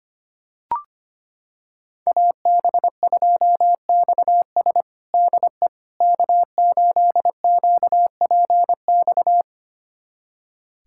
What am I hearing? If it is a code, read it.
Morse code, heard as a keyed tone, about 25 wpm, 700 Hz. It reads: AB2XS DE K8QPX
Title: USA Callsign DE Callsign - 5 Characters - 25wpm